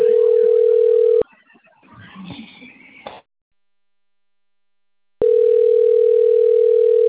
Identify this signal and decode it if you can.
Ringback tone (North America), 440 + 480 Hz.
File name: rg-1600-unknown- - - .15403